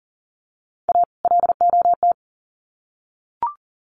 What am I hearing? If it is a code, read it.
Morse code, heard as a keyed tone, about 40 wpm, 700 Hz. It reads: A LOT
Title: Sets of 2 Words 40wpm